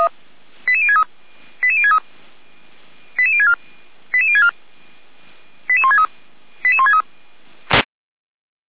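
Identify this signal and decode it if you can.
Five-tone selective call (ZVEI), heard as quick runs of five tones, tones of about 70 ms. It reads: 88063 88063 88064 88064 88163 88163
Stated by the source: ZVEI five-tone selective call 88521 received on an SDR.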